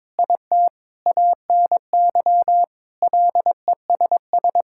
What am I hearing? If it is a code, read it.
Morse code, heard as a keyed tone, about 22 wpm, 700 Hz. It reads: IT ANY LESS
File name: practice / 22wpm / Sets of 7 Words - 22wpm